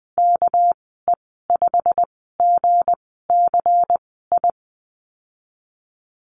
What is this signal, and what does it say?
Morse code, heard as a keyed tone, about 20 wpm, spaced as for 15 wpm, 700 Hz. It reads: KE5GCI
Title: 207 Lesson 33g - Letter j - Call Signs 20wpm